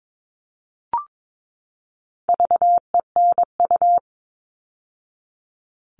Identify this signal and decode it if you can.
Morse code, heard as a keyed tone, about 22 wpm, 700 Hz. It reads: VENU
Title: All Cut Numbers - 4-Digit 22wpm